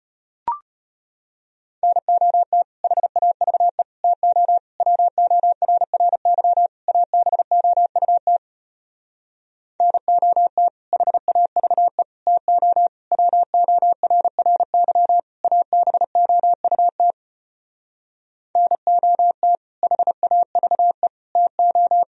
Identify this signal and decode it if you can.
Morse code, heard as a keyed tone, about 33 wpm, 700 Hz. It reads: NOT HAVE TO WORRY ABOUT NOT HAVE TO WORRY ABOUT NOT HAVE TO